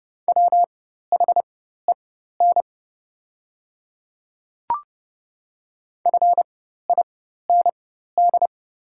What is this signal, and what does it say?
Morse code, heard as a keyed tone, about 30 wpm, spaced as for 14 wpm, 700 Hz. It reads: WHEN FIND
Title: 4x Character Spacing - Top 100 Words - 30wpm